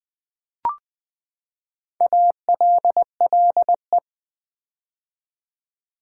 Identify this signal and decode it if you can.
Morse code, heard as a keyed tone, about 20 wpm, 700 Hz. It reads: ALLE
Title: Build A Word - 4 Letters At A Time - 10-Letter Words - 20wpm